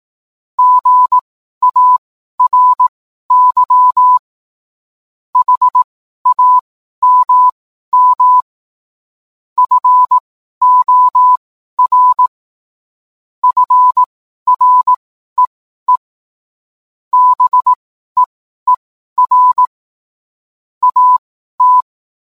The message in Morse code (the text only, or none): GARY HAMM FOR FREE BEER AT